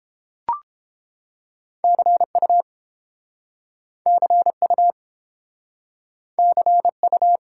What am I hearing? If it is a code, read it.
Morse code, heard as a keyed tone, about 30 wpm, 700 Hz. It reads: CU CU CU